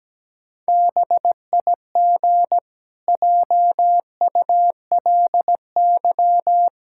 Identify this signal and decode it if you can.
Morse code, heard as a keyed tone, about 17 wpm, 700 Hz. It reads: BIG JULY
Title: Pangram Challenge v2 - 17wpm